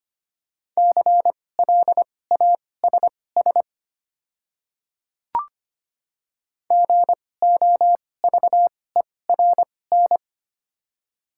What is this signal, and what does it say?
Morse code, heard as a keyed tone, about 25 wpm, spaced as for 18 wpm, 700 Hz. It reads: CLASS GOVERN
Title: 2x Character Spacing - Top 500 Words - 25wpm